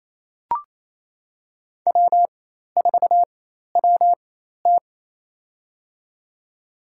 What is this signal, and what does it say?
Morse code, heard as a keyed tone, about 28 wpm, spaced as for 13 wpm, 700 Hz. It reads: W4WT
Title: USA Call Signs - 4x Character Spacing - 28wpm